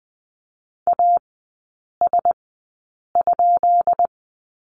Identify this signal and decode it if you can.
Morse code, heard as a keyed tone, about 20 wpm, spaced as for 8 wpm, 700 Hz. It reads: AS?
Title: 178 Lesson 29e - Slash - Three Characters 20wpm